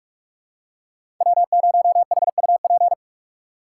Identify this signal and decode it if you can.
Morse code, heard as a keyed tone, about 44 wpm, 700 Hz. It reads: W0HUP
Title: USA Call Signs - 4x Character Spacing - 45wpm